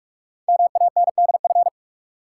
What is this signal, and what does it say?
Morse code, heard as a keyed tone, about 44 wpm, 700 Hz. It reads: MANDF
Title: Code Groups - 5 Letters - 45wpm